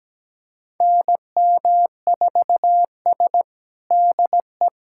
Morse code, heard as a keyed tone, about 17 wpm, 700 Hz. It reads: NM4S DE
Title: USA Callsign DE Callsign - 4 Characters - 17wpm